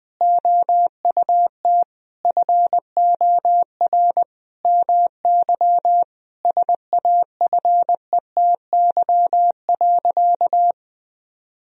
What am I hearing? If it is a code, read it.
Morse code, heard as a keyed tone, about 19 wpm, 700 Hz. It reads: OUT FOR MY SAFETY.
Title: Sentences from Top 1000 Words - Review Part 4 - 20wpm